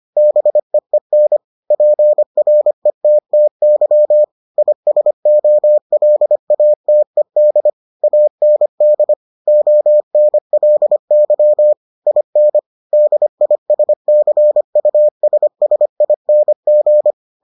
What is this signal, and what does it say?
Morse code, heard as a keyed tone, about 25 wpm, 600 Hz. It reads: BEEN PRETTY ISOLATED AND ONLY IN DISCUSSING